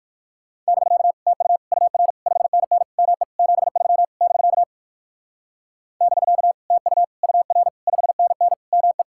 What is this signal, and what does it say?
Morse code, heard as a keyed tone, about 58 wpm, 700 Hz. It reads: <BK> TU UR 5NN ME 73 <BK> <BK> TU UR 5NN ME